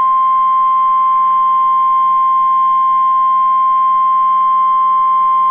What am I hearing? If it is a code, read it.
Emergency Alert System (SAME) tones (the 1050 Hz attention tone). It is heard from 0.7 to 5.2 s.